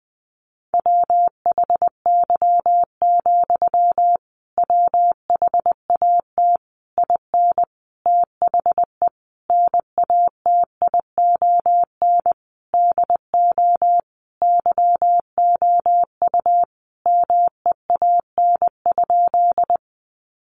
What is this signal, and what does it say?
Morse code, heard as a keyed tone, about 20 wpm, 700 Hz. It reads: WHY, WHAT IN THE NATION DO YOU MEAN?
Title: Sentences from Top 900 Words 20wpm